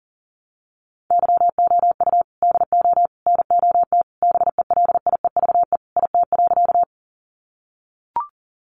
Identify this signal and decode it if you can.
Morse code, heard as a keyed tone, about 40 wpm, 700 Hz. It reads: YOU DO NOT BELIEVE IT.
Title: Sentences from Top 1000 Words 40wpm